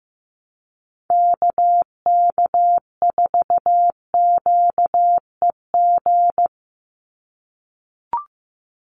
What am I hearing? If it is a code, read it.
Morse code, heard as a keyed tone, about 15 wpm, 700 Hz. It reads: KK4QEG
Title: USA Call Signs 15wpm